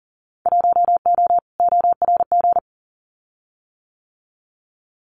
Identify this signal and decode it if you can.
Morse code, heard as a keyed tone, about 40 wpm, 700 Hz. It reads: 1O ORG
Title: Winter Field Day Exchanges 40wpm